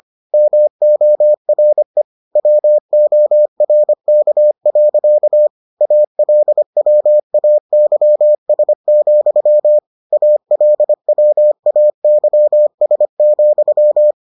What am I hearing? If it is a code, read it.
Morse code, heard as a keyed tone, about 25 wpm, 600 Hz. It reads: MORE WORK. ALWAYS, ALWAYS,